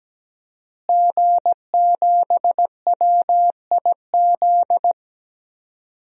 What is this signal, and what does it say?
Morse code, heard as a keyed tone, about 17 wpm, 700 Hz. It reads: G7WIZ
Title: DXCC-100 Call Signs - 5 Characters - Repeated 3x - 17wpm